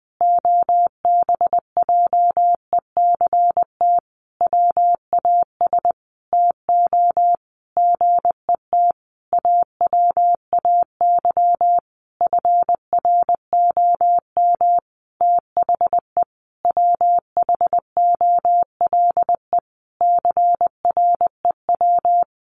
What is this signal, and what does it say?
Morse code, heard as a keyed tone, about 19 wpm, 700 Hz. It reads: OBJECT WAS TO GET AWAY FROM THE WHOLE CREW
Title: Sentences from Top 1500 Words 20wpm